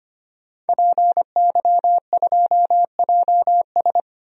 Morse code, heard as a keyed tone, about 25 wpm, 700 Hz. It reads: PY2JS
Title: DXCC-100 Call Signs - 5 Characters - 25wpm